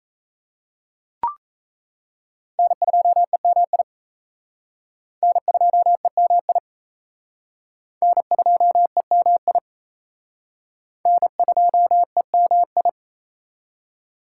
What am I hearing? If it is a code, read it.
Morse code, heard as a keyed tone, about 37 wpm, 700 Hz. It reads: N2EMI N2EMI N2EMI N2EMI